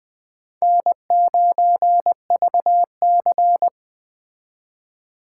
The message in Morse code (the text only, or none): N9VC